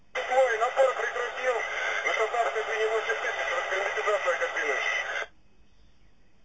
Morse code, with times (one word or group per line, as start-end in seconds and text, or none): none